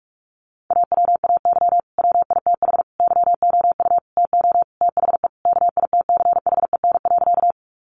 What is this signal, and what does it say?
Morse code, heard as a keyed tone, about 44 wpm, 700 Hz. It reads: AWAY WITH YOU TO THE KITCHEN.
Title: Sentences from Top 1400 Words 45wpm